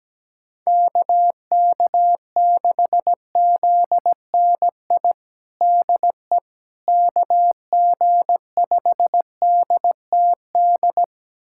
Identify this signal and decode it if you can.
Morse code, heard as a keyed tone, about 17 wpm, 700 Hz. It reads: KK6ZNI DE KG5DTD
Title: USA Callsign DE Callsign - 6 Characters - 17wpm